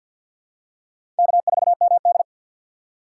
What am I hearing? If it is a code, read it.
Morse code, heard as a keyed tone, about 53 wpm, 700 Hz. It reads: K4MD
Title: USA Call Signs - 4 Characters - 3x Character Spacing - 50wpm